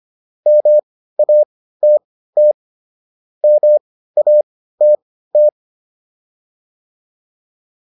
Morse code, heard as a keyed tone, about 25 wpm, spaced as for 15 wpm, 600 Hz. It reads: MATT MATT